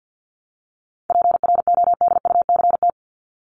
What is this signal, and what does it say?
Morse code, heard as a keyed tone, about 50 wpm, 700 Hz. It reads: PRODUCT
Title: ES:S Suffix - Top 100 Words with Base Words 50wpm